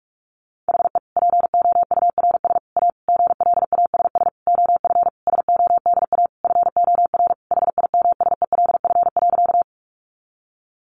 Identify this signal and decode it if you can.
Morse code, heard as a keyed tone, about 44 wpm, 700 Hz. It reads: HE POURS A GLASS OF SODA FOR HIMSELF.
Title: Additional Easy Phrases 45wpm